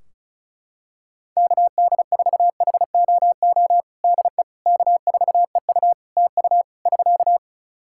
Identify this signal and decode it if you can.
Morse code, heard as a keyed tone, about 34 wpm, 700 Hz. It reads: KD4HOO DE K4EU TU <SK>
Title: Practice QSO - Exchange 4 - Ending - 35wpm